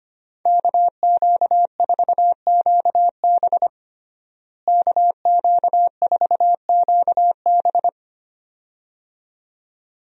Morse code, heard as a keyed tone, about 25 wpm, 700 Hz. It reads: KQ4QB KQ4QB
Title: USA Call Signs - 5 Characters - Repeated 2x - 25wpm